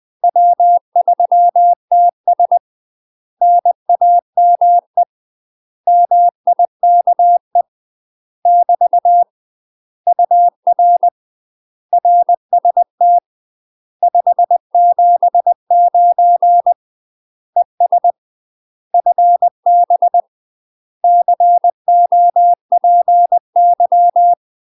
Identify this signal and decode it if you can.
Morse code, heard as a keyed tone, about 20 wpm, 700 Hz. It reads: W3TS NAME MIKE <BT> UR RST 579 ES FB COPY
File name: UK MASTER QSOs TRAINING / 20WPM / UK MASTER QSOs 0031